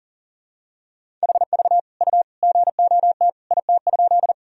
Morse code, heard as a keyed tone, about 37 wpm, 700 Hz. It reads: HV U GOT IT?
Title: Sentences from Top 300 Words and Abbreviations 40wpm